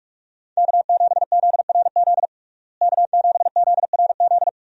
Morse code, heard as a keyed tone, about 46 wpm, 700 Hz. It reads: K7ZRZ K7ZRZ
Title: USA Call Signs - 5 Characters - Repeated 2x - 45wpm